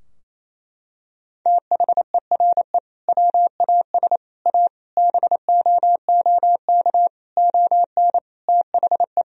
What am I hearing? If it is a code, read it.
Morse code, heard as a keyed tone, about 28 wpm, 700 Hz. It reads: THERE WAS A BOOK ON THE
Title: Sentences from Top 1000 Words - Review Part 3 - 28wpm